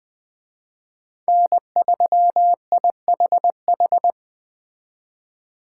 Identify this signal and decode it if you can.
Morse code, heard as a keyed tone, about 20 wpm, 700 Hz. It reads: N3IHH
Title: Speed Racing - USA 1x3 Call Signs 17wpm